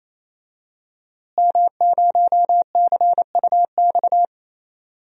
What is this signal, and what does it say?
Morse code, heard as a keyed tone, about 28 wpm, 700 Hz. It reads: M0CUX
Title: DXCC-100 Call Signs - 5 Characters - Repeated 3x - 28wpm